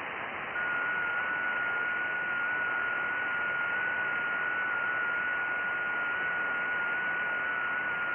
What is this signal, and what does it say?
FT8.
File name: IC-7100